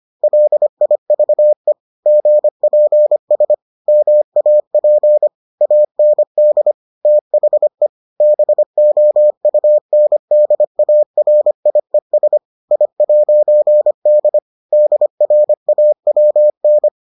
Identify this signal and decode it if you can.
Morse code, heard as a keyed tone, about 25 wpm, 600 Hz. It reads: LIVE GPS MAP AND THE BOUNDARIES I'D DRAWN